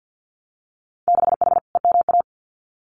Morse code, heard as a keyed tone, about 50 wpm, 700 Hz. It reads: 6H EMA
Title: Winter Field Day Exchanges Encore 50wpm